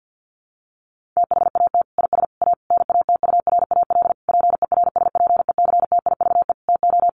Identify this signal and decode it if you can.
Morse code, heard as a keyed tone, about 50 wpm, 700 Hz. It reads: THAT IS A NATURAL PERSPECTIVE TO
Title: Sentences from Top 1200 Words 50wpm